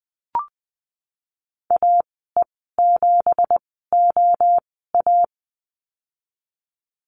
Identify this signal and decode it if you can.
Morse code, heard as a keyed tone, about 20 wpm, spaced as for 15 wpm, 700 Hz. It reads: AE7OA
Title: 179 Lesson 29g - Slash - Call Signs 20wpm